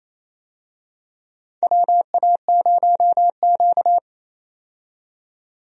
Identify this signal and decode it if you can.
Morse code, heard as a keyed tone, about 28 wpm, 700 Hz. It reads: WA0Q
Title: USA Call Signs - 4 Characters - 28wpm